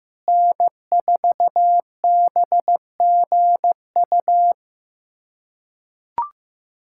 Morse code, heard as a keyed tone, about 15 wpm, 700 Hz. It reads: N4BGU
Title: USA Callsign DE Callsign - 5 Characters - 15wpm